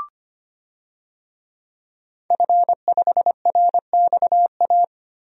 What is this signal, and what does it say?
Morse code, heard as a keyed tone, about 25 wpm, 700 Hz. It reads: F5RXA
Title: DXCC-100 Call Signs - 5 Characters - 25wpm